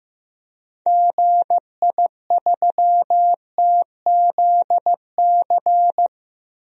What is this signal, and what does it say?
Morse code, heard as a keyed tone, about 15 wpm, 700 Hz. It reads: GI3TZC
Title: DXCC-100 Call Signs - 6 Characters - 15wpm